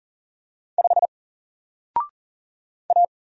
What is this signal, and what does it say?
Morse code, heard as a keyed tone, about 40 wpm, 700 Hz. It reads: 5 A
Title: Single Letter-Number Alt 40wpm